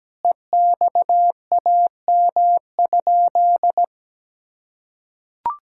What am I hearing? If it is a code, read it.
Morse code, heard as a keyed tone, about 17 wpm, 700 Hz. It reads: EXAM?
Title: Sentences from Top 1000 Words - Review Part 5 - 17wpm